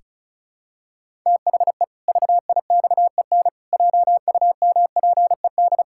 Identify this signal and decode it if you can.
Morse code, heard as a keyed tone, about 34 wpm, 700 Hz. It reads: THE VIXEN JUMPED
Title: Pangram Challenge v2 - 35wpm